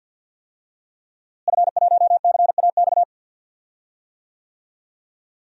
Morse code, heard as a keyed tone, about 52 wpm, 700 Hz. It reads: F1CAX
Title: DXCC-100 Call Signs - 5 Characters - Repeated 2x - 50wpm